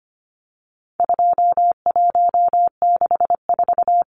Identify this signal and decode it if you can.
Morse code, heard as a keyed tone, about 25 wpm, 700 Hz. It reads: 2164
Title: Numbers - 4-Digit 25wpm